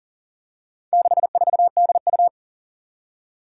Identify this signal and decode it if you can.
Morse code, heard as a keyed tone, about 39 wpm, 700 Hz. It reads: 64DU
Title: Four Challenging Letter-Number Combinations 40wpm